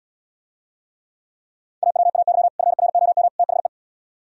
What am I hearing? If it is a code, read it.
Morse code, heard as a keyed tone, about 77 wpm, 700 Hz. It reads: AWAY FROM THE